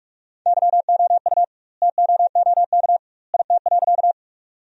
Morse code, heard as a keyed tone, about 43 wpm, 700 Hz. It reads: YOU TOOK IT.
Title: 2x Word-Spacing - Sentences from Top 300 Words - 45wpm